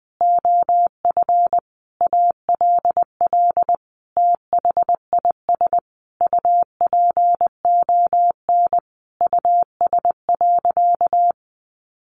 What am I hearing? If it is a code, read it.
Morse code, heard as a keyed tone, about 20 wpm, 700 Hz. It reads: OF ALL THIS UPON US.
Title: Sentences from Top 1400 Words 20wpm